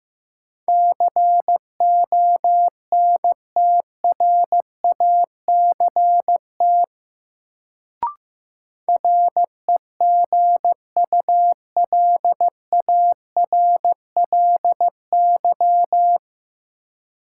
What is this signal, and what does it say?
Morse code, heard as a keyed tone, about 15 wpm, 700 Hz. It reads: CONTRACT REGULARLY
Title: Sentences from Top 1400 Words - New Word Drill - 15wpm